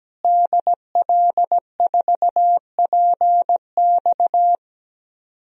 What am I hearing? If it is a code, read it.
Morse code, heard as a keyed tone, about 17 wpm, 700 Hz. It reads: DL4PX
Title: DXCC-100 Call Signs - 5 Characters - Repeated 3x - 17wpm